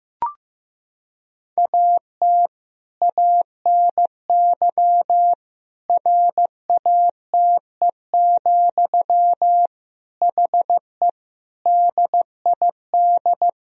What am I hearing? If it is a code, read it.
Morse code, heard as a keyed tone, about 16 wpm, 700 Hz. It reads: AT ANY RATE, HE DID
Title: Sentences from Top 1000 Words - Review Part 4 - 15wpm